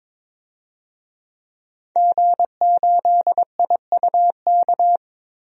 Morse code, heard as a keyed tone, about 22 wpm, 700 Hz. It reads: G8IUK
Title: DXCC-100 Call Signs - 5 Characters - Repeated 3x - 22wpm